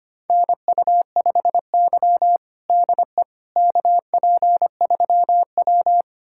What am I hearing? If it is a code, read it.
Morse code, heard as a keyed tone, about 25 wpm, 700 Hz. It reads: NU5Y DE KP3W
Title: USA Callsign DE Callsign - 4 Characters - 25wpm